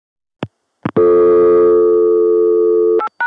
Dial tone (North America), 350 + 440 Hz.